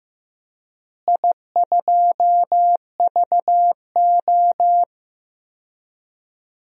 Morse code, heard as a keyed tone, about 16 wpm, 700 Hz. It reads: I2VO
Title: DXCC-100 Call Signs - 4 Characters - Repeated 2x - 15wpm